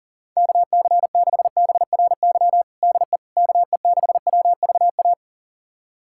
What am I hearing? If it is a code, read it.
Morse code, heard as a keyed tone, about 40 wpm, 700 Hz. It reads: KC6BRY DE KE6WVA
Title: USA Callsign DE Callsign - 6 Characters - 40wpm